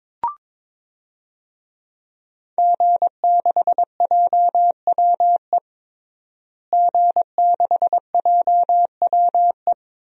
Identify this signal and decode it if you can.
Morse code, heard as a keyed tone, about 21 wpm, 700 Hz. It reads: G6JWE G6JWE
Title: DXCC-100 Call Signs - 5 Characters - Repeated 3x - 22wpm